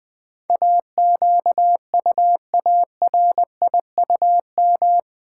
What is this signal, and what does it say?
Morse code, heard as a keyed tone, about 20 wpm, 700 Hz. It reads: AQUARIUM
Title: Difficult Words with Hints - 20wpm